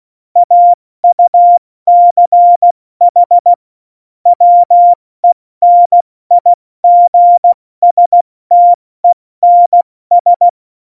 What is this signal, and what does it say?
Morse code, heard as a keyed tone, about 16 wpm, spaced as for 14 wpm, 700 Hz. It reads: AUCH WENIGSTENS
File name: Morsekurs / Witze / Witz 02. Juli 2019, 70 F80